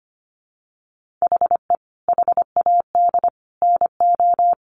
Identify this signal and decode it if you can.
Morse code, heard as a keyed tone, about 25 wpm, 700 Hz. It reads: HE HAD NO
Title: Sentences from Top 1400 Words 25wpm